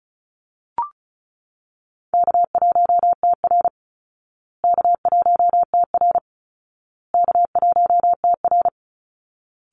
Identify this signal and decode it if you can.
Morse code, heard as a keyed tone, about 34 wpm, 700 Hz. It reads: K1TR K1TR K1TR
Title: Easy Call Signs Repeated 3x - 35wpm